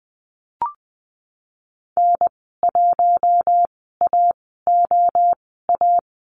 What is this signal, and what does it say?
Morse code, heard as a keyed tone, about 20 wpm, spaced as for 14 wpm, 700 Hz. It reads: N1AOA
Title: 048 Lesson 9g - Number 4 - Call Signs 20wpm